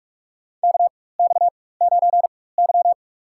Morse code, heard as a keyed tone, about 47 wpm, spaced as for 21 wpm, 700 Hz. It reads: KX9Y
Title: USA Call Signs - 4 Characters - 4x Character Spacing - 45wpm